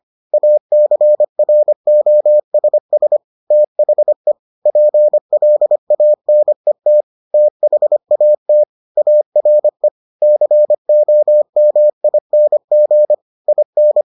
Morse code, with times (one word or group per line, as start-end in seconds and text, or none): 0.3-3.2 ACROSS
3.5-4.3 THE
4.7-7.0 PLANET
7.3-8.6 THAT
9.0-9.9 ARE
10.2-13.1 COMING
13.5-14.0 IN